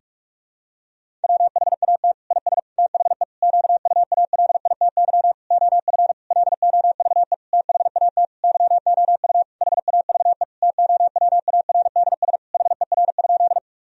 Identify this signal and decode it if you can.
Morse code, heard as a keyed tone, about 46 wpm, 700 Hz. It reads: WHAT IS THE QUALITY OF LOVE THAT YOU HAVE TOWARDS HER?
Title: Sentences from Top 1000 Words - Review Part 4 - 45wpm